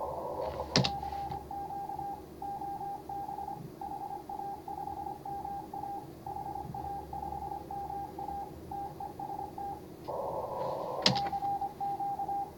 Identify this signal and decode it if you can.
Morse code, heard as a keyed tone, about 23 wpm, 800 Hz. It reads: CQ OB HA5MA HA5MA TEST CQ
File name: CW / HA5MA RHOB HANG CW 2017